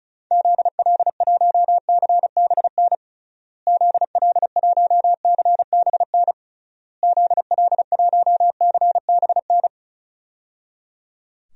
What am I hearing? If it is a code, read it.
Morse code, heard as a keyed tone, about 34 wpm, 700 Hz. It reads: ZL1CBN ZL1CBN ZL1CBN
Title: DXCC-100 Call Signs - 6 Characters - Repeated 3x - 35wpm